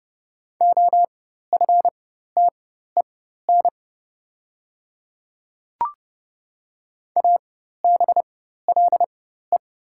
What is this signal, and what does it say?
Morse code, heard as a keyed tone, about 30 wpm, spaced as for 14 wpm, 690 Hz. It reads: OFTEN ABLE
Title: 4x Character Spacing - Top 500 Words - 30wpm